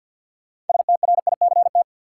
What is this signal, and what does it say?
Morse code, heard as a keyed tone, about 53 wpm, 700 Hz. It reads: STRICT
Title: Six Letter Words - 50wpm